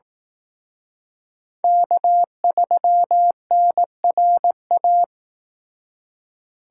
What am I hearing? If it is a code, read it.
Morse code, heard as a keyed tone, about 18 wpm, 700 Hz. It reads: K3NRA
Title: Speed Racing - USA 1x3 Call Signs 15wpm